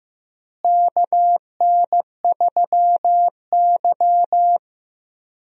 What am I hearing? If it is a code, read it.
Morse code, heard as a keyed tone, about 15 wpm, 700 Hz. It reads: KN3Y
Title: USA Call Signs - 4 Characters - Repeated 3x - 15wpm